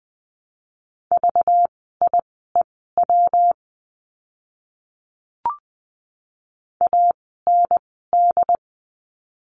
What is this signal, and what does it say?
Morse code, heard as a keyed tone, about 20 wpm, spaced as for 15 wpm, 700 Hz. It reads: VIEW AND
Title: 192 Lesson 31f - Letter v - Words 20wpm